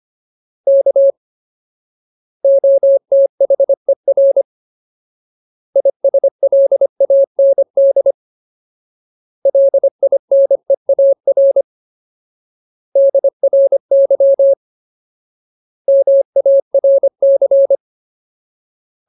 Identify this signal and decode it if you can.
Morse code, heard as a keyed tone, about 25 wpm, 550 Hz. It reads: K OTHER ISLAND LINEAR DRY MARC
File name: Ultimate Word List / 25WPM Ultimate